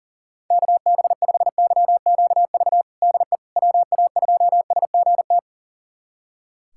Morse code, heard as a keyed tone, about 37 wpm, 700 Hz. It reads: KB5YQV DE WA2SGT
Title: USA Callsign DE Callsign 40wpm